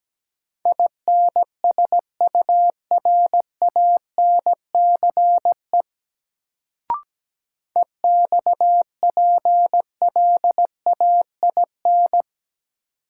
Morse code, heard as a keyed tone, about 17 wpm, 700 Hz. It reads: INSURANCE EXPLAIN
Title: Sentences from Top 1100 Words - New Word Drill - 17wpm